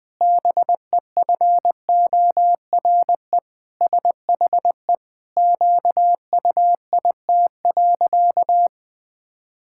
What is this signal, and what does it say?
Morse code, heard as a keyed tone, about 20 wpm, 700 Hz. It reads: BEFORE SHE QUIT.
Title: Pangram Challenge v2 - 20wpm